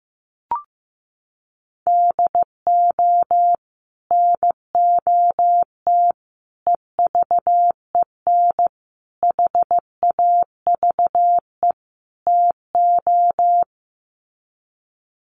Morse code, heard as a keyed tone, about 15 wpm, 700 Hz. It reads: DO NOT EVEN HAVE TO
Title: Sets of 5 Words Encore2 15wpm